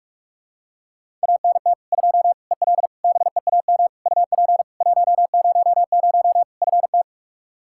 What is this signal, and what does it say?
Morse code, heard as a keyed tone, about 42 wpm, 700 Hz. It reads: ANT 2 EL BEAM UP 100 FT
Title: Practice QSO - Exchange 2 - Info - 45wpm